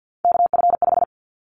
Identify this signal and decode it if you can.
Morse code, heard as a keyed tone, about 50 wpm, 700 Hz. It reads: KF5
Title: Callsign Prefixes for Top 25 DXCC - 50wpm